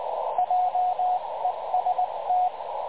Morse code, heard as a keyed tone, about 20 wpm, 730 Hz, with radio noise. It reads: JEST
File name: dlazaawansowanych / nagraniazpasma / websdr recording 3565.0kHz